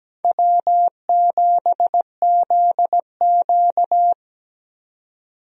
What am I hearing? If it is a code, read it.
Morse code, heard as a keyed tone, about 17 wpm, 700 Hz. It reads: W7ZQ
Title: USA Call Signs - 4 Characters - 17wpm